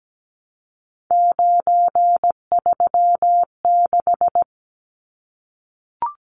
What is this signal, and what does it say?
Morse code, heard as a keyed tone, about 17 wpm, 700 Hz. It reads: 936
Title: Numbers - 3-Digit 17wpm